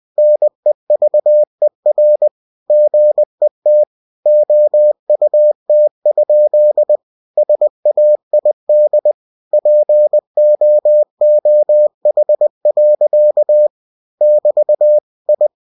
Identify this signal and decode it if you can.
Morse code, heard as a keyed tone, about 20 wpm, 600 Hz. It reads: NEVER GET OUT? SAID POOH. <BT> I